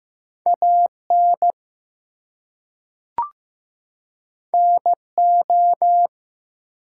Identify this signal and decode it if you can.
Morse code, heard as a keyed tone, about 15 wpm, 700 Hz. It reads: AN NO
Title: Two Letter Words - 15wpm